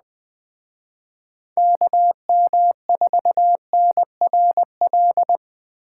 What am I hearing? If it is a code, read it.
Morse code, heard as a keyed tone, about 20 wpm, 700 Hz. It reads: KM4NRL